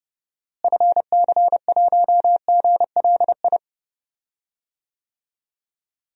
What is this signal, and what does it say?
Morse code, heard as a keyed tone, about 30 wpm, 700 Hz. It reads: FC1GLI
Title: DXCC-100 Call Signs - 6 Characters - Repeated 2x - 30wpm